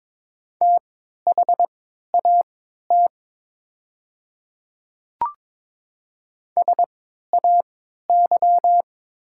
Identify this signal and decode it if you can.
Morse code, heard as a keyed tone, about 22 wpm, spaced as for 12 wpm, 700 Hz. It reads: THAT SAY
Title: 3x Character Spacing - Top 200 Words - 22wpm